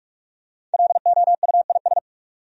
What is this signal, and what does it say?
Morse code, heard as a keyed tone, about 46 wpm, 700 Hz. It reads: LOUIS
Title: USA Common Names - Max 5 Letters - 45wpm